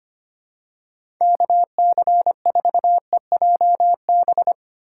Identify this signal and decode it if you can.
Morse code, heard as a keyed tone, about 25 wpm, 700 Hz. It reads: KC4EJB
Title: USA Call Signs - 6 Characters - 3x Character Spacing - 25wpm